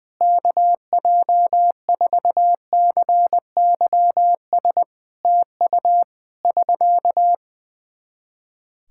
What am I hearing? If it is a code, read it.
Morse code, heard as a keyed tone, about 20 wpm, 700 Hz. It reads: KJ4CYS TU <SK>
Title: K1USN SST QSO Practice 20wpm